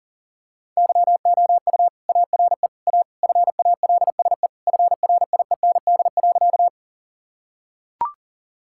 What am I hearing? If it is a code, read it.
Morse code, heard as a keyed tone, about 37 wpm, 700 Hz. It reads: YOU ARE A FALSE FRIEND.
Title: Sentences from Top 1600 Words 40wpm